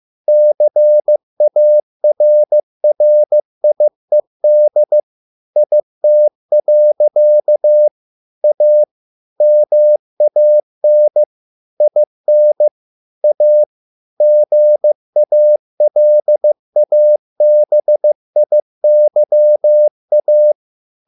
Morse code, heard as a keyed tone, about 15 wpm, 600 Hz. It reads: CARRIED IT. A MAN IN A GALABIYA